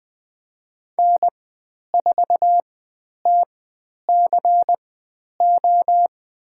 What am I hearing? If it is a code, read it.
Morse code, heard as a keyed tone, about 20 wpm, spaced as for 10 wpm, 700 Hz. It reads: N4TCO
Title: 095 Lesson 16g - Letter c - Call Signs 20wpm